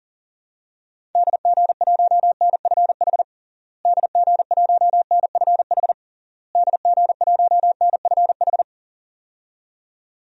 Morse code, heard as a keyed tone, about 40 wpm, 700 Hz. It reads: DG1NFH DG1NFH DG1NFH
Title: DXCC-100 Call Signs - 6 Characters - Repeated 3x - 40wpm